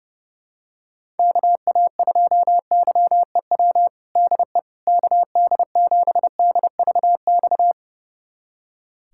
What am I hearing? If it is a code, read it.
Morse code, heard as a keyed tone, about 32 wpm, 700 Hz. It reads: KA2YEW DE KD7DVX
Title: USA Callsign DE Callsign - 6 Characters - 30wpm